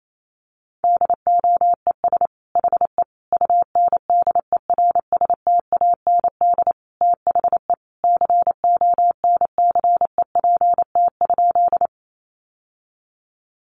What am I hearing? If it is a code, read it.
Morse code, heard as a keyed tone, about 28 wpm, 700 Hz. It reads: DOES HE UNDERSTAND THE CONCEPT?
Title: Sentences from Top 1100 Words 28wpm